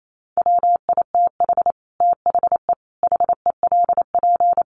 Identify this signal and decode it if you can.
Morse code, heard as a keyed tone, about 28 wpm, 700 Hz. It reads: WITH THE HELP
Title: Sets of 4 Words 28wpm